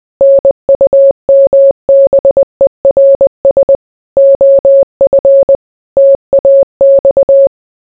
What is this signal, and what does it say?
Morse code, heard as a keyed tone, about 20 wpm, 550 Hz. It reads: NUMBERS OF TAX